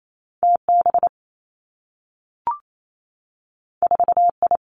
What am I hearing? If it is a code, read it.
Morse code, heard as a keyed tone, about 28 wpm, 700 Hz. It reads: T B 4 I
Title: Two Challenging Letter-Number Combinations 28wpm